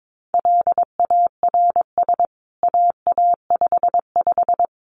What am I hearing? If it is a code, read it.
Morse code, heard as a keyed tone, about 22 wpm, 700 Hz. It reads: LARS AA55
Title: CWT Exchange 22wpm